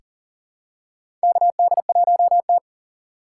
Morse code, heard as a keyed tone, about 38 wpm, 700 Hz. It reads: KD1T
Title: USA Call Signs - 4 Characters - 2x Character Spacing - 40wpm